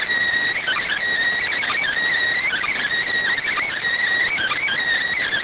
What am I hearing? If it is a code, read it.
Emergency Alert System (SAME) tones (header bursts), heard as harsh buzzing bursts. It is heard from 0.6 to 4.8 s.
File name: findings / CT 205 ax mips, iDose 3 0006.gen-42 rx